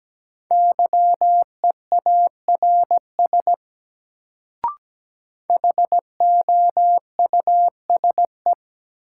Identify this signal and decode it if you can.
Morse code, heard as a keyed tone, about 17 wpm, 700 Hz. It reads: YEARS HOUSE
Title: Sentences from Top 200 Words - New Word Drill - 17wpm